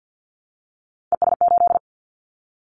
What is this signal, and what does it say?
Morse code, heard as a keyed tone, about 50 wpm, 700 Hz. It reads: ES8
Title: Callsign Prefixes for Top 50 DXCC - 50wpm